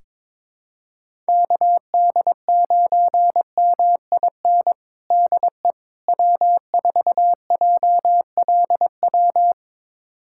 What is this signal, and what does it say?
Morse code, heard as a keyed tone, about 21 wpm, 700 Hz. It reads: KD9MIN DE W4JLW
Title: USA Callsign DE Callsign 22wpm